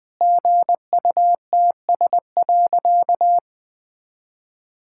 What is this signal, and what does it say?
Morse code, heard as a keyed tone, about 20 wpm, 700 Hz. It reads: GUTS.
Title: All-Age Appropriate Jokes - 20wpm